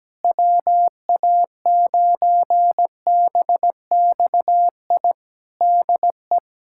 Morse code, heard as a keyed tone, about 17 wpm, 700 Hz. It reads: WA9BXI DE
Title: K1USN SST QSO Practice 17wpm